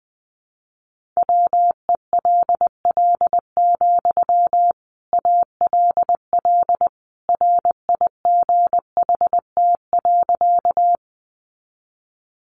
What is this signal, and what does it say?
Morse code, heard as a keyed tone, about 20 wpm, 700 Hz. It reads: WELL, ALL RIGHT.
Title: Sentences from Top 200 Words 20wpm